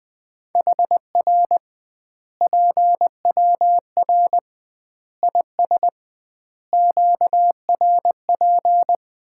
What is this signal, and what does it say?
Morse code, heard as a keyed tone, about 20 wpm, 700 Hz. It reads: HR PWR IS QRP
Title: Common QSO Phrases - 2x Word-Spacing - 20wpm